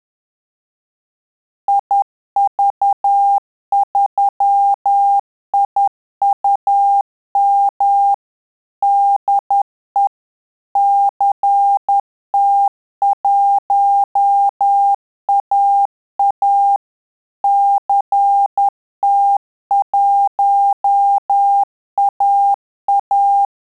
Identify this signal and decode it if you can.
Morse code, heard as a keyed tone, about 10 wpm, 800 Hz. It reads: IV3IUM DE CT1AA CT1AA
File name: Telegraphy-group / QSO CW 10 PPM